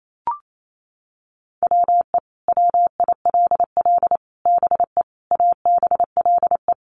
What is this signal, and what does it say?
Morse code, heard as a keyed tone, about 28 wpm, 700 Hz. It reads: WE WILL BE ABLE
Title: Sets of 5 Words Encore 28wpm